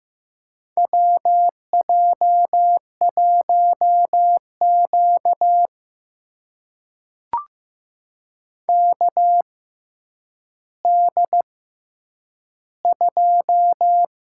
Morse code, heard as a keyed tone, about 15 wpm, spaced as for 9 wpm, 700 Hz. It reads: WJ1Q KD2
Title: USA Call Signs - 6x Character Spacing - 15wpm